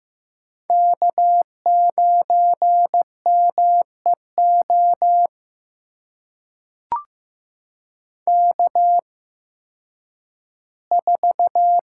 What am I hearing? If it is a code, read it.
Morse code, heard as a keyed tone, about 15 wpm, spaced as for 10 wpm, 700 Hz. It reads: K9MEO K4
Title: USA Call Signs - 5 Characters - 8x Character Spacing - 15wpm